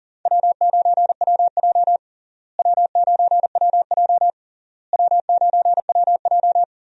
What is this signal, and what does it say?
Morse code, heard as a keyed tone, about 40 wpm, 700 Hz. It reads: W9WJ W9WJ W9WJ
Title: USA Call Signs - 4 Characters - Repeated 3x - 40wpm